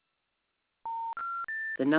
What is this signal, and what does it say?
Special information tone, with speech.